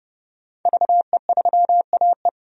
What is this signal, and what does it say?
Morse code, heard as a keyed tone, about 30 wpm, 700 Hz. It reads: VE3AE
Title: DXCC-100 Call Signs - 5 Characters - 30wpm